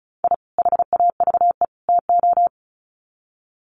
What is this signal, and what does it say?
Morse code, heard as a keyed tone, about 34 wpm, 700 Hz. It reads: I HAVE TO
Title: Sets of 3 Words 35wpm